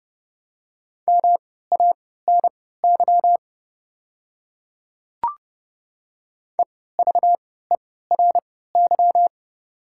Morse code, heard as a keyed tone, about 30 wpm, spaced as for 17 wpm, 700 Hz. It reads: MANY EVERY
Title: 3x Character Spacing - Top 300 Words - 30wpm